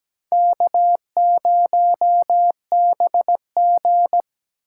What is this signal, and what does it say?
Morse code, heard as a keyed tone, about 20 wpm, 700 Hz. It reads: K0BG
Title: Speed Racing - USA Call Signs 17wpm